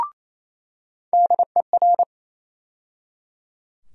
Morse code, heard as a keyed tone, about 28 wpm, 700 Hz. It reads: DER
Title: USA Call Sign Postfixes - 3 Letters 28wpm